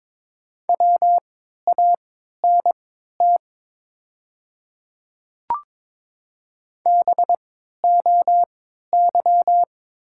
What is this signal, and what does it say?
Morse code, heard as a keyed tone, about 22 wpm, spaced as for 12 wpm, 700 Hz. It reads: WANT BOY
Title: 3x Character Spacing - Top 300 Words - 22wpm